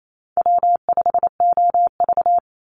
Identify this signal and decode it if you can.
Morse code, heard as a keyed tone, about 28 wpm, 700 Hz. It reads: W5OV
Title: Easy Call Signs Repeated 3x - 28wpm